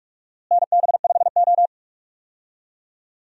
Morse code, heard as a keyed tone, about 46 wpm, 700 Hz. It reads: NB5O
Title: USA Call Signs - 4 Characters - Repeated 2x - 45wpm